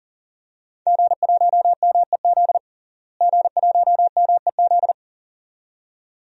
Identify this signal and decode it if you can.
Morse code, heard as a keyed tone, about 40 wpm, 700 Hz. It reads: G1MEZ G1MEZ
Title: DXCC-100 Call Signs - 5 Characters - Repeated 2x - 40wpm